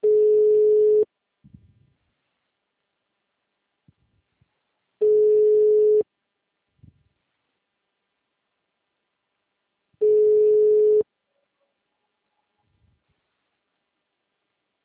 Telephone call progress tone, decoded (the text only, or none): ringback tone